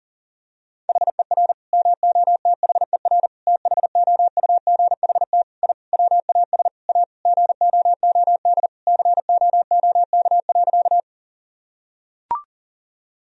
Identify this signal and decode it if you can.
Morse code, heard as a keyed tone, about 40 wpm, 700 Hz. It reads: HER MOTHER THOUGHT I WAS A GOOD COOK.
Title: Sentences from Top 1000 Words - Review Part 4 - 40wpm